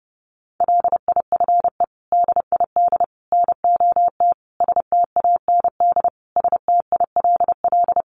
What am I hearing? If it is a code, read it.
Morse code, heard as a keyed tone, about 30 wpm, 700 Hz. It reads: LIFE DID NOT STAND STILL
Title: Sentences from Top 800 Words 30wpm